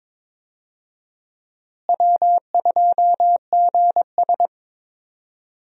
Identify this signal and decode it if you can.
Morse code, heard as a keyed tone, about 22 wpm, 700 Hz. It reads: W2GS
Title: USA Call Signs - 4 Characters - Repeated 2x - 22wpm